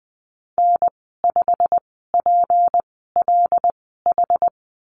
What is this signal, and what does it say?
Morse code, heard as a keyed tone, about 20 wpm, spaced as for 15 wpm, 700 Hz. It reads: N5PLH
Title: 161 Lesson 26g - Letter g - Call Signs 20wpm